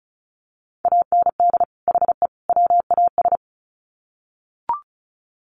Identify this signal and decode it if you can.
Morse code, heard as a keyed tone, about 34 wpm, 700 Hz. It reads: AND HE WAS
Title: Sets of 3 Words 35wpm